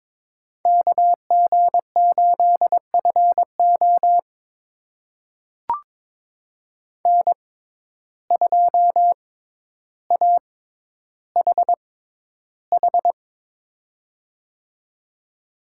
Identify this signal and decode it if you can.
Morse code, heard as a keyed tone, about 23 wpm, spaced as for 11 wpm, 700 Hz. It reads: KG8FO N2AHH
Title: USA Call Signs - 5 Characters - 6x Character Spacing - 22wpm